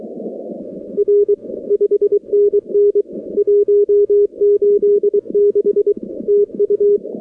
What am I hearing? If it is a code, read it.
Morse code, heard as a keyed tone, about 23 wpm, 410 Hz. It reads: R 5NN 186 TU
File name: contests / 2019-cw-wpx-kt5j-k5pi- -2330 0-46